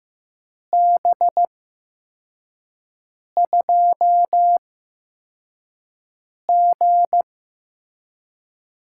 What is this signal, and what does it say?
Morse code, heard as a keyed tone, about 15 wpm, spaced as for 4 wpm, 700 Hz. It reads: B2G
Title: USA Call Signs - 8x Character Spacing - 15wpm